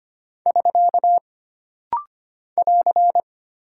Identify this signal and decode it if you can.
Morse code, heard as a keyed tone, about 25 wpm, 700 Hz. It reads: <SK> <AR>
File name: Single Punctuation-Prosign - Rapid-Fire - 25wpm